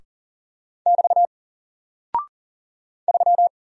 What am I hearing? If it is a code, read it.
Morse code, heard as a keyed tone, about 40 wpm, 700 Hz. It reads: <BT> 3
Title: Single Character (Letter:Number:Punctuation:Prosign) 40wpm